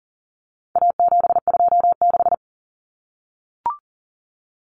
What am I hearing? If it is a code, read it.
Morse code, heard as a keyed tone, about 40 wpm, 700 Hz. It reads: A726
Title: Common Cut Numbers - 4-Digit 40wpm